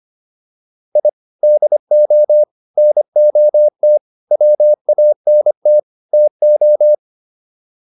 Morse code, heard as a keyed tone, about 25 wpm, 600 Hz. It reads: I DO NOT WANT TO